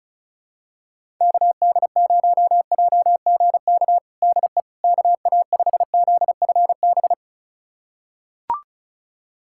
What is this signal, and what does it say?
Morse code, heard as a keyed tone, about 34 wpm, 700 Hz. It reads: KD0JGK DE KA5ZFB
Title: USA Callsign DE Callsign - 6 Characters - 35wpm